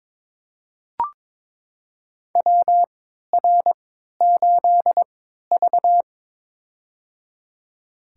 Morse code, heard as a keyed tone, about 22 wpm, spaced as for 13 wpm, 700 Hz. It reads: WR8V
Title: USA Call Signs - 4 Characters - 3x Character Spacing - 22wpm